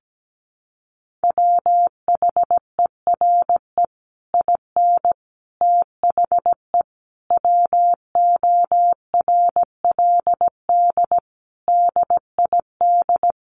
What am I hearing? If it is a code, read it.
Morse code, heard as a keyed tone, about 17 wpm, 700 Hz. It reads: WHERE IN THE WORLD DID
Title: Sentences from Top 200 Words 17wpm